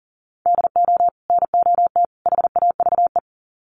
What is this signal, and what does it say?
Morse code, heard as a keyed tone, about 40 wpm, 700 Hz. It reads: DO NOT HAVE
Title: Sets of 3 Words 40wpm